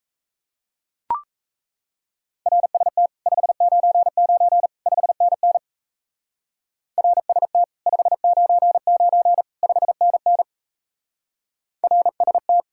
Morse code, heard as a keyed tone, about 36 wpm, 700 Hz. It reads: RST 599 5NN RST 599 5NN RST